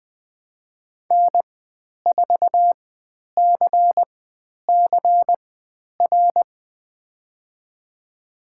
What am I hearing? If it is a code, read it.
Morse code, heard as a keyed tone, about 20 wpm, spaced as for 10 wpm, 700 Hz. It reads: N4CCR
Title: 095 Lesson 16g - Letter c - Call Signs 20wpm